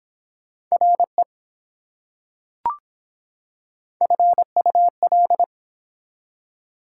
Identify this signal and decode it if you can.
Morse code, heard as a keyed tone, about 24 wpm, 700 Hz. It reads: RE FUL